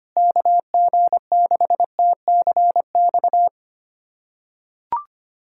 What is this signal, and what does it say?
Morse code, heard as a keyed tone, about 25 wpm, 700 Hz. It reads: KG6TCX
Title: USA Callsign DE Callsign 25wpm